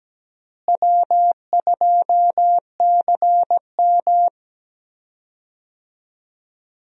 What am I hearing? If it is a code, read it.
Morse code, heard as a keyed tone, about 17 wpm, 700 Hz. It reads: W2CM
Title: USA Call Signs - 4 Characters - Repeated 2x - 17wpm